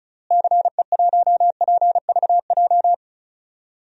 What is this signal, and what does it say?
Morse code, heard as a keyed tone, about 36 wpm, 700 Hz. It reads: CE1PVJ
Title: DXCC-100 Call Signs - 6 Characters - Repeated 3x - 35wpm